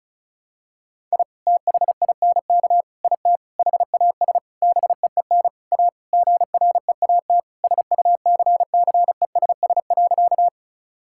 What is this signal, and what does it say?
Morse code, heard as a keyed tone, about 36 wpm, 700 Hz. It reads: I THINK IT HAS BEEN A GREAT SUCCESS.
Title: Sentences from Top 1000 Words - Review Part 6 - 35wpm